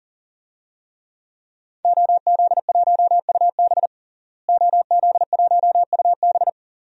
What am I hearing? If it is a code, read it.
Morse code, heard as a keyed tone, about 40 wpm, 700 Hz. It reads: OZ1UB OZ1UB
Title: DXCC-100 Call Signs - 5 Characters - Repeated 2x - 40wpm